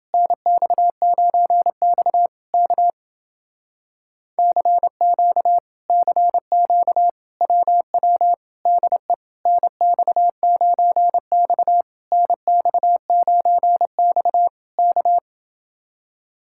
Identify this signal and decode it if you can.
Morse code, heard as a keyed tone, about 30 wpm, 700 Hz. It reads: NX9X K CQ CQ WW DE NX9X NX9X K